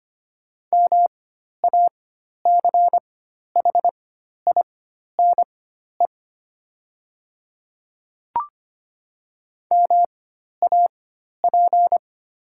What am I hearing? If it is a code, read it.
Morse code, heard as a keyed tone, about 25 wpm, spaced as for 12 wpm, 700 Hz. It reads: MACHINE MAP
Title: 4x Character Spacing - Top 500 Words - 25wpm